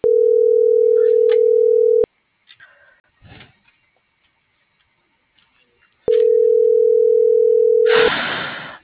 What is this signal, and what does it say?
Ringback tone (North America), 440 + 480 Hz.